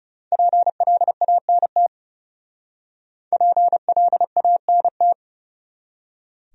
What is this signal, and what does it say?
Morse code, heard as a keyed tone, about 35 wpm, 700 Hz. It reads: PLANT PLANT
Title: Speed-Racing Top 500 Words 30wpm